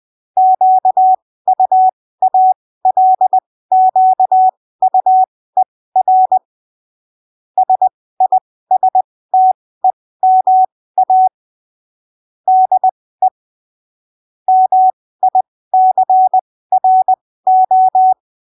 Morse code, heard as a keyed tone, about 20 wpm, spaced as for 15 wpm, 750 Hz. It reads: QUALQUER SISTEMA DE MICRO
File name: texto 09 20ppm